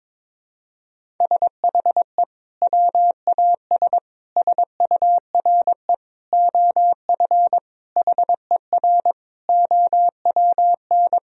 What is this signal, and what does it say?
Morse code, heard as a keyed tone, about 21 wpm, 700 Hz. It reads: SHE WAS SURE OF HER OWN
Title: Sentences from Top 1000 Words - Review Part 1 - 22wpm